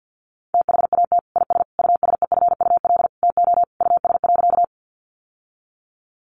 Morse code, heard as a keyed tone, about 50 wpm, 700 Hz. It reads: THAT IS USEFUL TO US.
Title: Sentences from Top 700 Words 50wpm